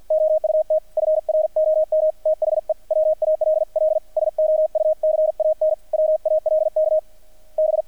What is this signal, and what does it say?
Morse code, heard as a keyed tone, about 50 wpm, 630 Hz, with radio noise. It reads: OUT FROM THE WALL SO U CAN WALK B
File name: archived / CRSnet